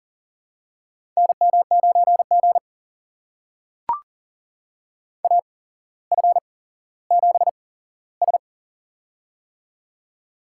Morse code, heard as a keyed tone, about 40 wpm, spaced as for 17 wpm, 700 Hz. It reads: NM9G AF7S
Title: USA Call Signs - 4 Characters - 8x Character Spacing - 40wpm